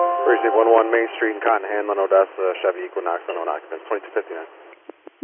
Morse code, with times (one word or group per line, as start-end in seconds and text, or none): none